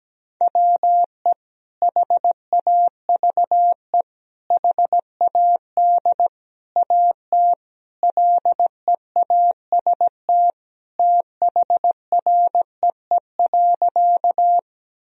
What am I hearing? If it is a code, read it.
Morse code, heard as a keyed tone, about 17 wpm, 700 Hz. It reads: WE HAVE HAD AT LEAST THREE.
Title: Sentences from Top 1000 Words - Review Part 1 - 17wpm